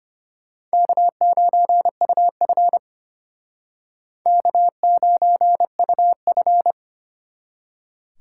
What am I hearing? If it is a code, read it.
Morse code, heard as a keyed tone, about 32 wpm, 700 Hz. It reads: K9UF K9UF